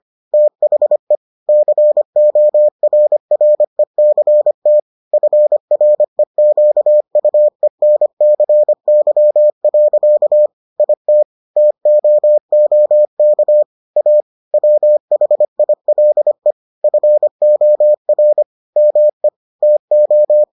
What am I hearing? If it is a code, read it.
Morse code, heard as a keyed tone, about 25 wpm, 600 Hz. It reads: THE CORRECT FREQUENCY. IT TOOK A WHILE FOR ME TO